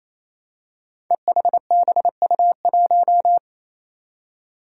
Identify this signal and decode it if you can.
Morse code, heard as a keyed tone, about 27 wpm, 700 Hz. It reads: EHBU1
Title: Five Challenging Letter-Number Combinations 28wpm